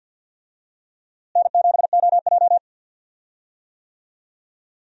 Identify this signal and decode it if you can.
Morse code, heard as a keyed tone, about 50 wpm, 700 Hz. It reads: N7OJ
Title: USA Call Signs - 4 Characters - Repeated 2x - 50wpm